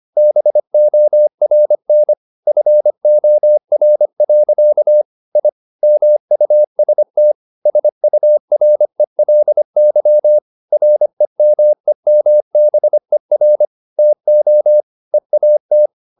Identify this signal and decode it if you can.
Morse code, heard as a keyed tone, about 25 wpm, 600 Hz. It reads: BORN FOR. I MUST SURELY REMEMBER TO EAT
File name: the old man and the sea part 1 25wpm